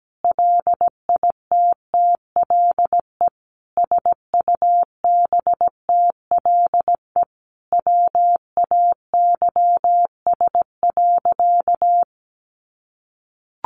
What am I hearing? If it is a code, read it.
Morse code, heard as a keyed tone, about 17 wpm, 700 Hz. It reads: LITTLE SUBTLE WAYS.
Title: A Sound Of Thunder 17wpm